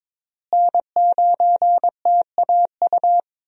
Morse code, heard as a keyed tone, about 22 wpm, 700 Hz. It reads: N9TAU
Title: USA Call Signs - 5 Characters - Repeated 2x - 22wpm